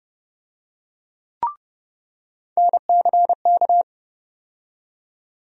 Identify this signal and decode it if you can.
Morse code, heard as a keyed tone, about 30 wpm, 700 Hz. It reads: NCK
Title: USA Call Sign Postfixes - 3 Letters 30wpm